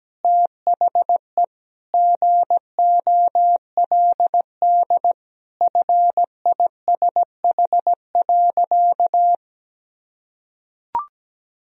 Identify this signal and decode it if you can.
Morse code, heard as a keyed tone, about 17 wpm, 700 Hz. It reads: THE GOLD FISH.
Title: Sentences from Top 1000 Words - Review Part 3 - 17wpm